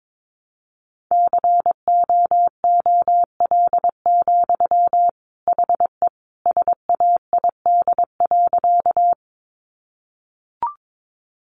Morse code, heard as a keyed tone, about 22 wpm, 700 Hz. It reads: COOL, HE SAID.
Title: Sentences from Top 1200 Words 22wpm